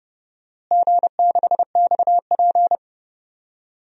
Morse code, heard as a keyed tone, about 32 wpm, 700 Hz. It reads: G6XP
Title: DXCC-100 Call Signs - 4 Characters - Repeated 3x - 30wpm